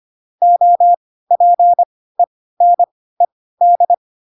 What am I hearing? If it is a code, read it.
Morse code, heard as a keyed tone, about 25 wpm, spaced as for 16 wpm, 700 Hz. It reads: OPENED
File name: Pinocchio / Pinocchio ch1 10 25 ik1hge